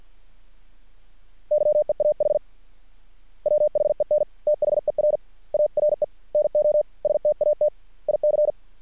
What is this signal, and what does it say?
Morse code, heard as a keyed tone, about 50 wpm, spaced as for 44 wpm, 600 Hz. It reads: YEAH WHEN THER ARE NO STATIC
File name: archived / CRSnet